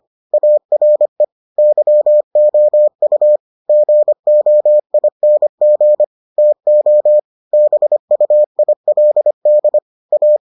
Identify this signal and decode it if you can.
Morse code, heard as a keyed tone, about 25 wpm, 600 Hz. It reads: ARE YOU GOING TO BUILD A